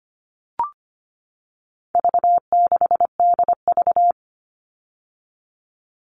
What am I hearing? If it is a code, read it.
Morse code, heard as a keyed tone, about 25 wpm, 700 Hz. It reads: V6DV
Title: All Cut Numbers - 4-Digit 25wpm